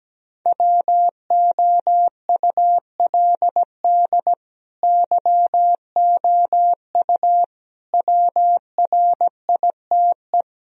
Morse code, heard as a keyed tone, about 17 wpm, 700 Hz. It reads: WOULD YOU WRITE
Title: Sentences from Top 1000 Words - Review Part 2 - 17wpm